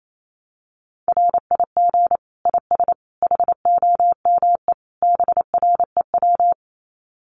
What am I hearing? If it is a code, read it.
Morse code, heard as a keyed tone, about 28 wpm, 700 Hz. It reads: RIG IS HOME BREW
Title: Common QSO Phrases 28wpm